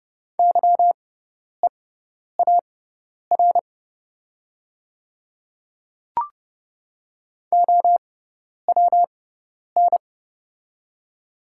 Morse code, heard as a keyed tone, about 30 wpm, spaced as for 10 wpm, 690 Hz. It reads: YEAR OWN
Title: 6x Character Spacing - Top 300 Words - 30wpm